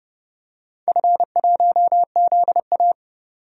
Morse code, heard as a keyed tone, about 29 wpm, 700 Hz. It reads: F1ZA
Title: DXCC-100 Call Signs - 4 Characters - 30wpm